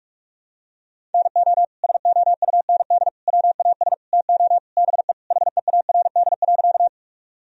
Morse code, heard as a keyed tone, about 48 wpm, 700 Hz. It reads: NO SOUND WAS TO BE HEARD.
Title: Sentences from Top 1000 Words - Review Part 1 - 45wpm